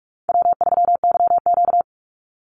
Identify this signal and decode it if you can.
Morse code, heard as a keyed tone, about 44 wpm, 700 Hz. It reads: W3YQ
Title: Easy Call Signs Repeated 2x - 45wpm